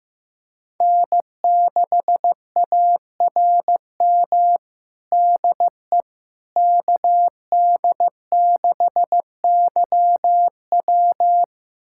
Morse code, heard as a keyed tone, about 15 wpm, 700 Hz. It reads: N6ARM DE KD6YW
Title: USA Callsign DE Callsign - 5 Characters - 15wpm